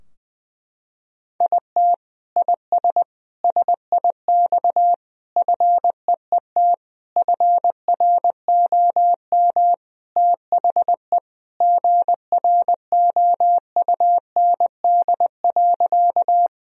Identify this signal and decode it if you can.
Morse code, heard as a keyed tone, about 20 wpm, 700 Hz. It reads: IT IS SIX FEET FROM THE GROUND.
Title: Sentences from Top 1000 Words - Review Part 2 - 20wpm